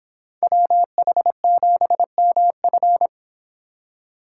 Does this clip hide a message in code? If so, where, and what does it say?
Yes: Morse code from 0.4 to 3.1 s, heard as a keyed tone, about 26 wpm, 700 Hz. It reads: WH7MF